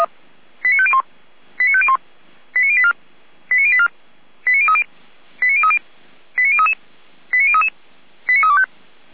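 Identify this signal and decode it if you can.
Five-tone selective call (ZVEI), heard as quick runs of five tones, tones of about 70 ms. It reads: 89501 89501 89084 89084 89030 89030 89033 89033 89325
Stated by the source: ZVEI five-tone selective call 89140 received on SDR, from the feuerwehr-orlamuende fire brigade audio server.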